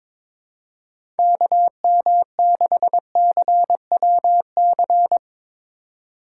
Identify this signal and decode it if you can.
Morse code, heard as a keyed tone, about 22 wpm, 700 Hz. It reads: KM6CWC
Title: USA Call Signs - 6 Characters - 2x Character Spacing - 22wpm